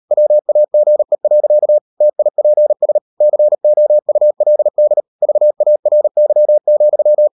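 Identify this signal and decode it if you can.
Morse code, heard as a keyed tone, about 38 wpm, 600 Hz. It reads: WAGE. TIPS COULD VARY,
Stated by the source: Morse code practice at 38 WPM, sending Ken Follett's novel Never in English, from CW med Gnister.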